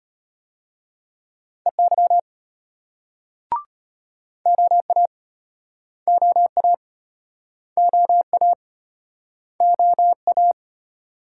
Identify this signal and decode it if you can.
Morse code, heard as a keyed tone, about 36 wpm, 700 Hz. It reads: EY OA OA OA OA